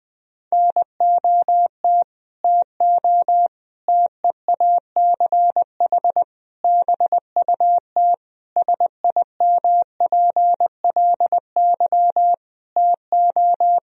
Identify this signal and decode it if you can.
Morse code, heard as a keyed tone, about 21 wpm, 700 Hz. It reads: NOT TO TEACH BUT SIMPLY TO
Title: Sentences from Top 1000 Words - Review Part 4 - 20wpm